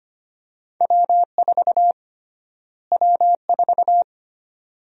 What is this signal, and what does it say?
Morse code, heard as a keyed tone, about 25 wpm, 700 Hz. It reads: W4 W4
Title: USA Call Sign Prefixes - Repeated 2x - 25wpm